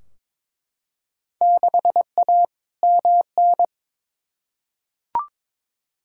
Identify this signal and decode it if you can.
Morse code, heard as a keyed tone, about 21 wpm, 700 Hz. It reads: 6A MN
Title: Field Day Exchanges Encore - 22wpm